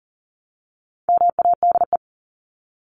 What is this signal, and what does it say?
Morse code, heard as a keyed tone, about 40 wpm, 700 Hz. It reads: MADE
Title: Top 500 Words 40wpm